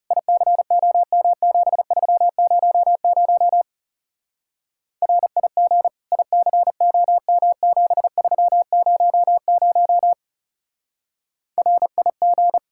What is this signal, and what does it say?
Morse code, heard as a keyed tone, about 36 wpm, 700 Hz. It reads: ICOM7300 RIG ICOM7300 RIG
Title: Speed-Racing Common QSO Phrases - 30wpm